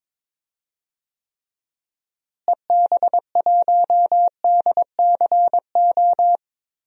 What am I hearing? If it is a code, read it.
Morse code, heard as a keyed tone, about 22 wpm, 700 Hz. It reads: EB1DCO
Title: DXCC-100 Call Signs - 6 Characters - Repeated 3x - 22wpm